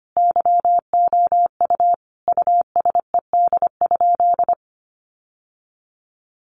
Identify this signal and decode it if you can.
Morse code, heard as a keyed tone, about 25 wpm, 700 Hz. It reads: YOU USED?
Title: Sentences from Top 1200 Words 25wpm